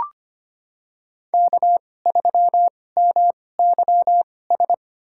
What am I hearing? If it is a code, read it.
Morse code, heard as a keyed tone, about 25 wpm, spaced as for 18 wpm, 700 Hz. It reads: K3MYS
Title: USA Call Signs - 5 Characters - 2x Character Spacing - 25wpm